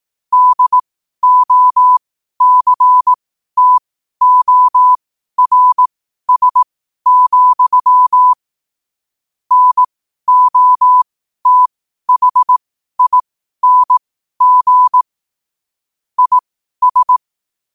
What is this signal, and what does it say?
Morse code, heard as a keyed tone, about 18 wpm, spaced as for 12 wpm, 1000 Hz. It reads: DOCTORS, NOTHING IS
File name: qotdfiles / 12wpm